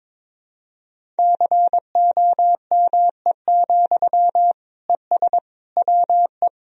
Morse code, heard as a keyed tone, about 22 wpm, 700 Hz. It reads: COME, ES WE
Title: Sentences from Top 500 Words and Abbreviations 22wpm